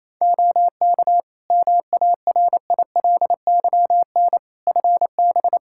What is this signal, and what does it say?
Morse code, heard as a keyed tone, about 28 wpm, 700 Hz. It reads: OK MARILYN FB
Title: Practice QSO - Exchange 2 - Info - 28wpm